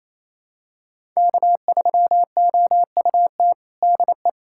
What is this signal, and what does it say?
Morse code, heard as a keyed tone, about 28 wpm, 700 Hz. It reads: K3OUT DE
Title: USA Callsign DE Callsign - 5 Characters - 28wpm